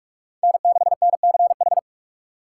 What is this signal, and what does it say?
Morse code, heard as a keyed tone, about 42 wpm, 700 Hz. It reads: N6NCH
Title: USA Call Signs - 6x Character Spacing - 45wpm